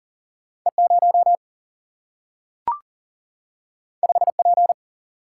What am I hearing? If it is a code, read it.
Morse code, heard as a keyed tone, about 40 wpm, 700 Hz. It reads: E 0 5 P
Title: Single-Letter Single-Number 40wpm